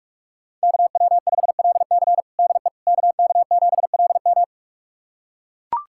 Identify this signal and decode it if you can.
Morse code, heard as a keyed tone, about 43 wpm, 700 Hz. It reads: KW5LC DE KK7LM
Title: USA Callsign DE Callsign - 5 Characters - 45wpm